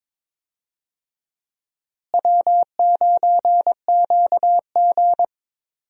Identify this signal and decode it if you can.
Morse code, heard as a keyed tone, about 22 wpm, 700 Hz. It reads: W9QG
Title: USA Call Signs - 4 Characters - Repeated 2x - 22wpm